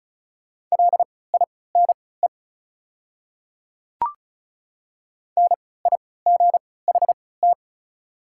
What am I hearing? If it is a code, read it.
Morse code, heard as a keyed tone, about 36 wpm, spaced as for 20 wpm, 690 Hz. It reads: LINE NIGHT
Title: 3x Character Spacing - Top 500 Words - 35wpm